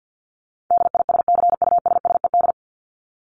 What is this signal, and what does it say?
Morse code, heard as a keyed tone, about 50 wpm, 700 Hz. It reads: DISCUSSED
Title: ED Suffix - Top 100 Words with Base Words 50wpm